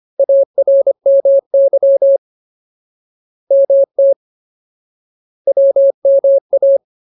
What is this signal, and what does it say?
Morse code, heard as a keyed tone, about 25 wpm, 550 Hz. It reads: ARMY MT WMA
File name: Ultimate Word List / 25WPM Ultimate